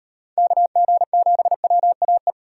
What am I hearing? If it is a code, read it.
Morse code, heard as a keyed tone, about 38 wpm, 700 Hz. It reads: KG7WAE